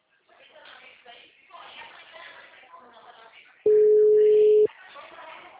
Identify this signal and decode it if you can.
Ringback tone (Europe, 425 Hz), with speech.
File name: out- -2018- - - .48639